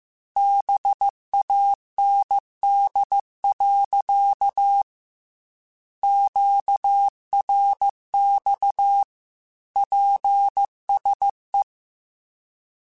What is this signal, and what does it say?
Morse code, heard as a keyed tone, about 15 wpm, 800 Hz. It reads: BAND. QRX PSE
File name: QSOs / QSOs-mp3s / kc7sac@13